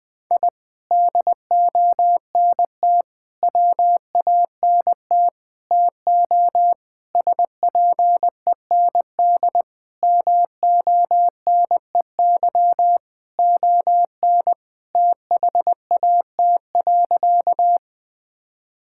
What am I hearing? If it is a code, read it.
Morse code, heard as a keyed tone, about 20 wpm, 700 Hz. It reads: I DONT WANT TO SPEND MONEY ON THAT.
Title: Sentences from Top 1000 Words - Review Part 5 - 20wpm